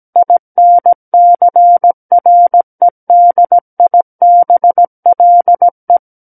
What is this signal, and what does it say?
Morse code, heard as a keyed tone, about 17 wpm, 700 Hz. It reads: INCREDIBLE
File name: Book One / 08-Chapter 8 - 17 WPM